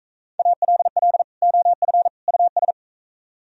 Morse code, heard as a keyed tone, about 39 wpm, 700 Hz. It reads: ALL OF US